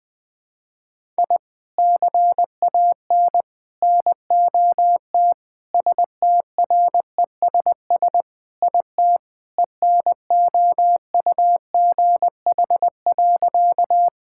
Morse code, heard as a keyed tone, about 20 wpm, 700 Hz. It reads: I CAN NOT STRESS IT ENOUGH.
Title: Sentences from Top 1000 Words - Review Part 6 - 20wpm